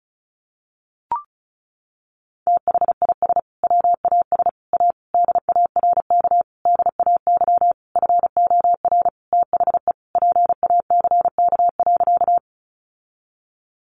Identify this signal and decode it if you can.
Morse code, heard as a keyed tone, about 34 wpm, 700 Hz. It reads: THIS WAS A DARK DAY FOR THE PACK.
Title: Sentences from Top 1300 Words 35wpm